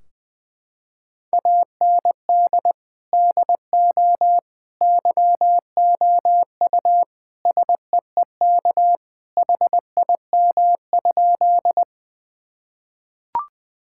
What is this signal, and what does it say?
Morse code, heard as a keyed tone, about 20 wpm, 700 Hz. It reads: AND DO YOU SEEK HIM?
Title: Sentences from Top 1600 Words 20wpm